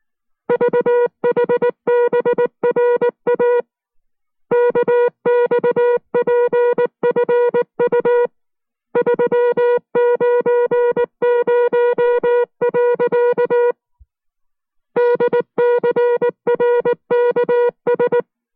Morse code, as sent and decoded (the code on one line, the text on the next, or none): ...- .... -... .-. .- / -.- -..- .--. ..-. ..- / ...-- ----. ----- .-.-.- / -.. -.-. .-. -.- ...
VHBRA KXPFU 390. DCRKS